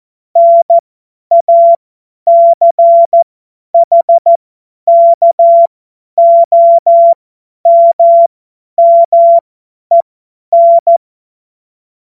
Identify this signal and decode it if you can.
Morse code, heard as a keyed tone, about 14 wpm, spaced as for 10 wpm, 680 Hz. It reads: NACHKOMMEN
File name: Morsekurs / Witze / Witz 28.November 2023 50 F70